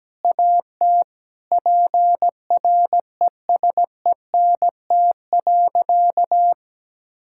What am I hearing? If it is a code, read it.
Morse code, heard as a keyed tone, about 17 wpm, 700 Hz. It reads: AT PRESENT.
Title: Sentences from Top 1000 Words - Review Part 5 - 17wpm